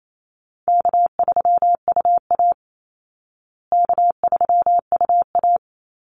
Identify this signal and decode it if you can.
Morse code, heard as a keyed tone, about 28 wpm, 700 Hz. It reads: K3UA K3UA
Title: Call Signs Repeated 2x - 28wpm